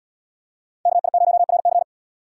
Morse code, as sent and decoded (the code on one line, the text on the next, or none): -.- .. ----- .-. -.-.
KI0RC